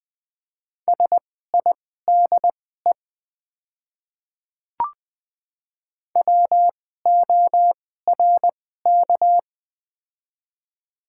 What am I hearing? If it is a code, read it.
Morse code, heard as a keyed tone, about 20 wpm, spaced as for 14 wpm, 700 Hz. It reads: SIDE WORK
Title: 2x Character Spacing - Top 300 Words - 20wpm